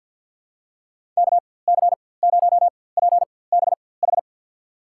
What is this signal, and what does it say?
Morse code, heard as a keyed tone, about 48 wpm, spaced as for 23 wpm, 700 Hz. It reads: KC0PBH
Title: USA Call Signs - 6 Characters - 4x Character Spacing - 50wpm